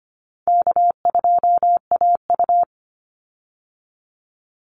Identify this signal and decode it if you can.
Morse code, heard as a keyed tone, about 25 wpm, 700 Hz. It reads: K2AU
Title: Easy Call Signs Repeated 3x - 25wpm